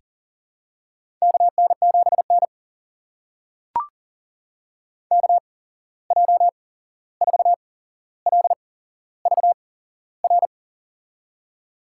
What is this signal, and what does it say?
Morse code, heard as a keyed tone, about 40 wpm, spaced as for 15 wpm, 700 Hz. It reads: KN7N KJ4LVR
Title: USA Call Signs - 8x Character Spacing - 40wpm